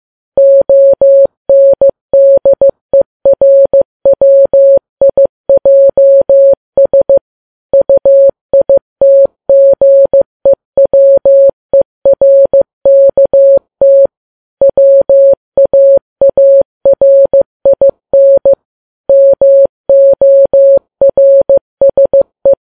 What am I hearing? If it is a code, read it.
Morse code, heard as a keyed tone, about 15 wpm, 550 Hz. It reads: ONDERWIJS UITGEWERKT WAARIN MORSE
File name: Files Web tls / Morse on9fv / morse15kps